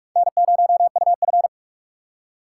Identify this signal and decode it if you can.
Morse code, heard as a keyed tone, about 41 wpm, 700 Hz. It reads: N0UF